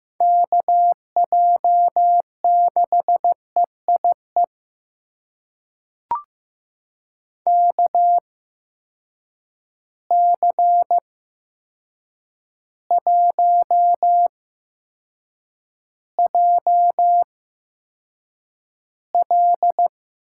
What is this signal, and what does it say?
Morse code, heard as a keyed tone, about 15 wpm, spaced as for 7 wpm, 700 Hz. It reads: KJ6EIE KC1JL
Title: USA Call Signs - 6 Characters - 8x Character Spacing - 15wpm